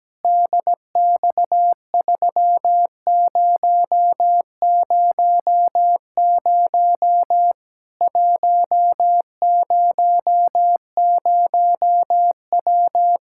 Morse code, heard as a keyed tone, about 17 wpm, 700 Hz. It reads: DX3000 100W
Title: Practice QSO - Exchange 2 - Info - 17wpm